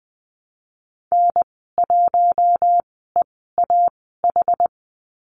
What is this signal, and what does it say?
Morse code, heard as a keyed tone, about 20 wpm, spaced as for 14 wpm, 700 Hz. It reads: N1EAH
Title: 062 Lesson 11g - Letter h - Call Signs 20wpm